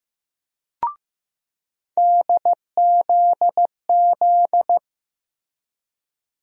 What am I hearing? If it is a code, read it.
Morse code, heard as a keyed tone, about 15 wpm, 700 Hz. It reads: DZZ
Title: USA Call Sign Postfixes - 3 Letters 15wpm